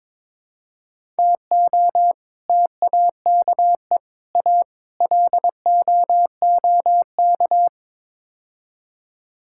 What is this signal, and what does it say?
Morse code, heard as a keyed tone, about 21 wpm, 700 Hz. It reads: TO TAKE A LOOK
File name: practice / 22wpm / Build A Phrase - 5-Word Phrases - 22wpm